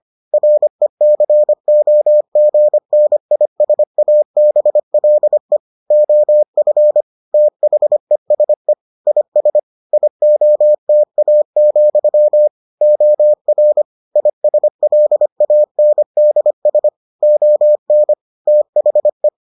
Morse code, heard as a keyed tone, about 25 wpm, 600 Hz. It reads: RECOGNISABLE OF THESE IS IOTA, OR ISLANDS ON THE